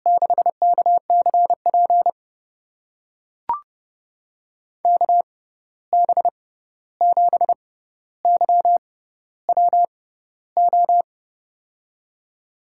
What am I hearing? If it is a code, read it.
Morse code, heard as a keyed tone, about 32 wpm, spaced as for 14 wpm, 700 Hz. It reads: 6KCP KB7YWO
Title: USA Call Signs - 6 Characters - 6x Character Spacing - 30wpm